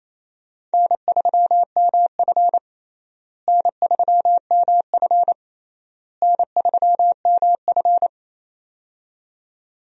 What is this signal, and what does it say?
Morse code, heard as a keyed tone, about 28 wpm, 700 Hz. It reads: N3MF N3MF N3MF
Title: USA Call Signs - 4 Characters - Repeated 3x - 28wpm